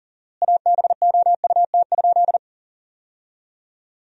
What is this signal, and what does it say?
Morse code, heard as a keyed tone, about 40 wpm, 700 Hz. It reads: ABOUT?
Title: 4x Word-Spacing - Sentences from Top 100 Words - 40wpm